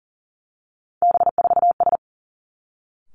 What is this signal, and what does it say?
Morse code, heard as a keyed tone, about 42 wpm, 700 Hz. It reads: B4S
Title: Three Challenging Letter-Number Combinations 40wpm